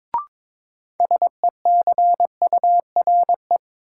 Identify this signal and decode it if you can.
Morse code, heard as a keyed tone, about 22 wpm, 700 Hz. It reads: SECURE
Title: Sentences from Top 1200 Words - New Word Drill Alt - 22wpm